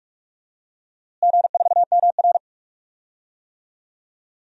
Morse code, heard as a keyed tone, about 41 wpm, 700 Hz. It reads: G4MR
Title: DXCC-100 Call Signs - 4 Characters - Repeated 3x - 45wpm